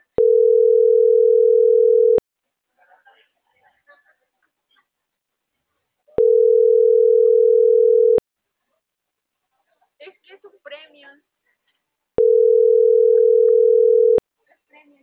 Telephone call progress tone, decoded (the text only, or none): ringback tone